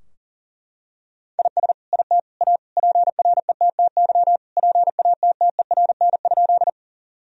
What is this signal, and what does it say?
Morse code, heard as a keyed tone, about 40 wpm, 700 Hz. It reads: IS IT A PRETTY PATTERN?
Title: Sentences from Top 1000 Words - Review Part 2 - 40wpm